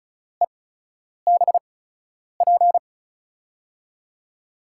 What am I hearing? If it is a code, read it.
Morse code, heard as a keyed tone, about 34 wpm, spaced as for 10 wpm, 700 Hz. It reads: EBP
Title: USA Call Signs - 5 Characters - 8x Character Spacing - 35wpm